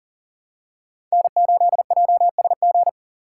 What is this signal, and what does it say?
Morse code, heard as a keyed tone, about 40 wpm, 700 Hz. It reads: N8JSG
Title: USA Call Signs - 5 Characters - 4x Character Spacing - 40wpm